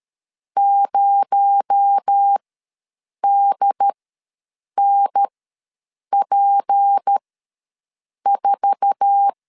Morse code, heard as a keyed tone, about 13 wpm, spaced as for 7 wpm, 800 Hz. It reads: 0DNP4
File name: LEZIONI / 8 YJ45 / 8 YJ4 ws 12wpm 3249